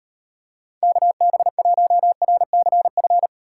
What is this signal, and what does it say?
Morse code, heard as a keyed tone, about 38 wpm, 700 Hz. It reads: KB1RCF